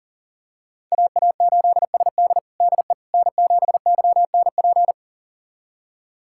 Morse code, heard as a keyed tone, about 40 wpm, 700 Hz. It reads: AA8SD DE N7YNP
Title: USA Callsign DE Callsign - 5 Characters - 40wpm